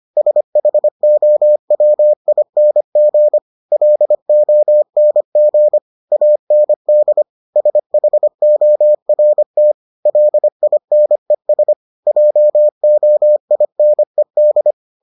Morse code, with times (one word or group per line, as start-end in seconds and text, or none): 0.2-3.4 SHOWING
3.7-5.8 LONG
6.1-7.2 AND
7.6-9.7 SHORT
10.1-11.7 LINES
12.1-14.7 JOINED